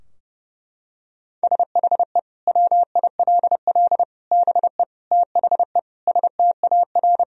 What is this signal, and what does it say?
Morse code, heard as a keyed tone, about 32 wpm, 700 Hz. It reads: SHE WILL BE THE STAR
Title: Sentences from Top 1000 Words - Review Part 3 - 30wpm